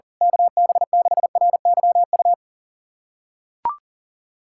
Morse code, heard as a keyed tone, about 40 wpm, 700 Hz. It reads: KB6RYU
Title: USA Callsign DE Callsign - 6 Characters - 40wpm